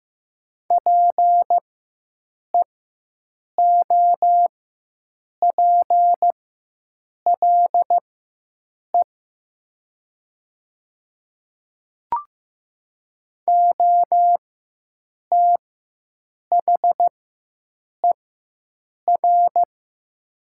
Morse code, heard as a keyed tone, about 15 wpm, spaced as for 7 wpm, 700 Hz. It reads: PEOPLE OTHER
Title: 4x Character Spacing - Top 100 Words - 15wpm